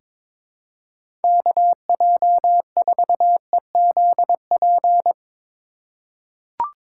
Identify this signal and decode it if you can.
Morse code, heard as a keyed tone, about 22 wpm, 700 Hz. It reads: KJ4EZP
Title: USA Call Signs - 6 Characters - 2x Character Spacing - 22wpm